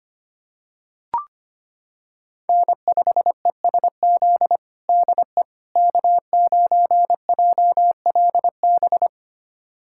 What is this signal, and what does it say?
Morse code, heard as a keyed tone, about 25 wpm, 700 Hz. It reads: N5ESZ DE K9JLB
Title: USA Callsign DE Callsign - 5 Characters - 25wpm